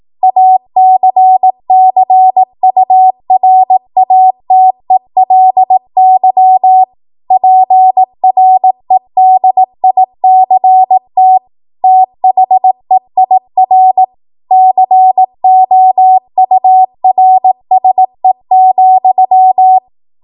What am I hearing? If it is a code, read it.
Morse code, heard as a keyed tone, about 18 wpm, 750 Hz. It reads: ACCURATELY PREDICT THEIR COURSE,
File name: Morse / 18WPM